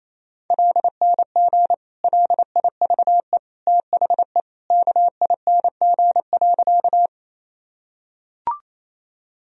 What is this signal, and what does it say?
Morse code, heard as a keyed tone, about 27 wpm, 700 Hz. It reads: LNG LIVE THE KING.
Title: Sentences from Top 400 Words and Abbreviations 28wpm